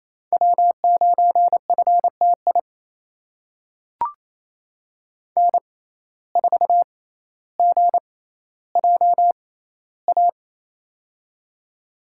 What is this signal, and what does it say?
Morse code, heard as a keyed tone, about 28 wpm, spaced as for 14 wpm, 700 Hz. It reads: W9FTI N4GJA
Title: USA Call Signs - 5 Characters - 6x Character Spacing - 28wpm